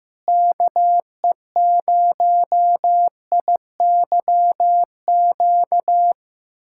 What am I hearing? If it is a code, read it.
Morse code, heard as a keyed tone, about 15 wpm, 700 Hz. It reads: KE0IYQ
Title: USA Callsign DE Callsign - 6 Characters - 15wpm